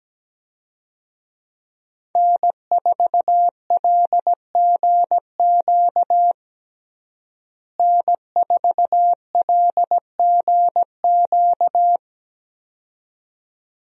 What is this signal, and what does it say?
Morse code, heard as a keyed tone, about 17 wpm, 700 Hz. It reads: N4LGQ N4LGQ
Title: USA Call Signs - 5 Characters - Repeated 2x - 17wpm